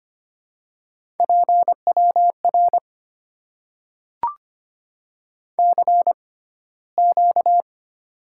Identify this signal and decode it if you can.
Morse code, heard as a keyed tone, about 25 wpm, spaced as for 15 wpm, 700 Hz. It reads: PWR CQ
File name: Top 90 QSO Elements - 6x Character Spacing - 25wpm